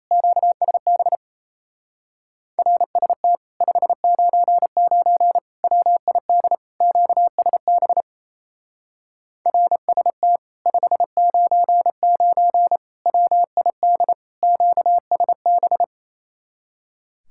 Morse code, heard as a keyed tone, about 34 wpm, 700 Hz. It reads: QSB RST 599 WID QSB RST 599 WID QSB
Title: Speed-Racing Common QSO Phrases Encore - 28wpm